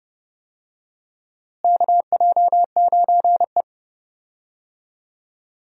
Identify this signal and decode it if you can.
Morse code, heard as a keyed tone, about 32 wpm, 700 Hz. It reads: KJ9E
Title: USA Call Signs - 4 Characters - Repeated 3x - 30wpm